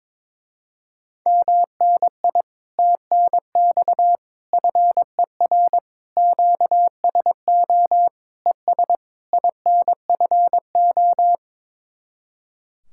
Morse code, heard as a keyed tone, about 22 wpm, 700 Hz. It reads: MNI TNX FER QSO ES INFO
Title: Speed-Racing Common QSO Phrases - 22wpm